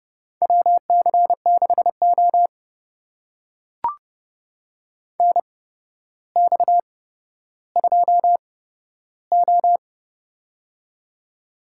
Morse code, heard as a keyed tone, about 30 wpm, spaced as for 13 wpm, 700 Hz. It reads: WC6O NX2O
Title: USA Call Signs - 4 Characters - 8x Character Spacing - 30wpm